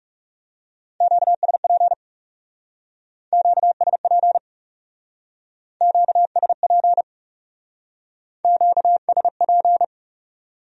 Morse code, heard as a keyed tone, about 37 wpm, 700 Hz. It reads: QSP QSP QSP QSP
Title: Speed-Racing Q-Codes - 30wpm